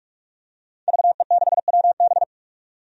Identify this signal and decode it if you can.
Morse code, heard as a keyed tone, about 42 wpm, 700 Hz. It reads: VE6WB
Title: DXCC-100 Call Signs - 5 Characters - 45wpm